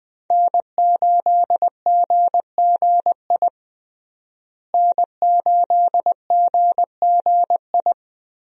Morse code, heard as a keyed tone, about 21 wpm, 700 Hz. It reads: N8GGI N8GGI
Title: USA Call Signs - 5 Characters - Repeated 2x - 20wpm